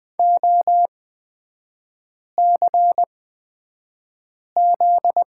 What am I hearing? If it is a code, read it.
Morse code, heard as a keyed tone, about 20 wpm, spaced as for 5 wpm, 700 Hz. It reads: OCZ
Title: 250 Lesson 40e - BK - Three Characters 20wpm